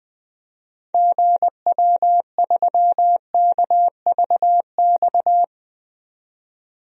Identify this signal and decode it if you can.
Morse code, heard as a keyed tone, about 20 wpm, 700 Hz. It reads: GW3KVX
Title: DXCC-100 Call Signs - 6 Characters - 20wpm